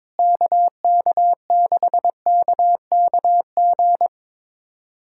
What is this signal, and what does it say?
Morse code, heard as a keyed tone, about 22 wpm, 700 Hz. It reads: KK6KKG
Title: USA Callsign DE Callsign - 6 Characters - 22wpm